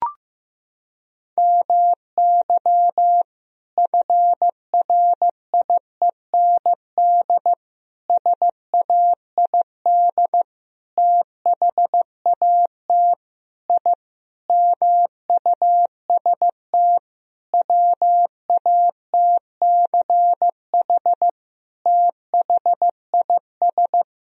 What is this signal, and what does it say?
Morse code, heard as a keyed tone, about 15 wpm, 700 Hz. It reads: MY FRIEND SAID THAT I MUST WATCH THIS
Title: Sentences from Top 1000 Words - Review Part 4 - 15wpm